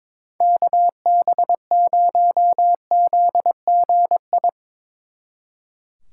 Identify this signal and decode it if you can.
Morse code, heard as a keyed tone, about 24 wpm, 700 Hz. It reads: KB0ZGI
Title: Speed Racing - USA 2x3 Call Signs 22wpm